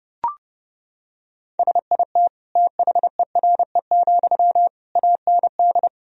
Morse code, heard as a keyed tone, about 30 wpm, 700 Hz. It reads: SIT THERE, AND
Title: Sentences from Top 1000 Words - Review Part 1 - 30wpm